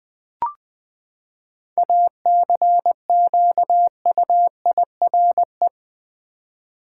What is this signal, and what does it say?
Morse code, heard as a keyed tone, about 20 wpm, 700 Hz. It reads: ACQUIRE
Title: Seven Letter Words - 20wpm